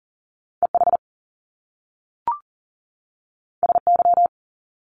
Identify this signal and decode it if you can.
Morse code, heard as a keyed tone, about 38 wpm, 700 Hz. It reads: EH SY
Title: Top Two Letter Combinations in Words 40wpm